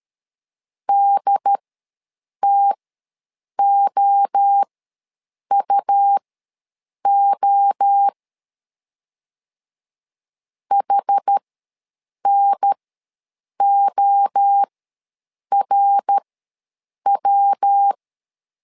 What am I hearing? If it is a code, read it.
Morse code, heard as a keyed tone, about 13 wpm, spaced as for 7 wpm, 800 Hz. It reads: DTOUO HNORW
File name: LEZIONI / 3 MWRO / 3 MWRO ws 12wpm 1914